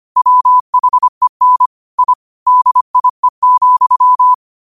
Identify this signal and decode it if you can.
Morse code, heard as a keyed tone, about 25 wpm, 1000 Hz. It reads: WHEN I DIE,
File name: qotdfiles / 25wpm